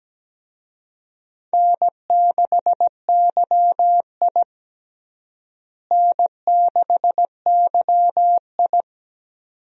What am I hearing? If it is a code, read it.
Morse code, heard as a keyed tone, about 17 wpm, 700 Hz. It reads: N6YI N6YI
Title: USA Call Signs - 4 Characters - Repeated 2x - 17wpm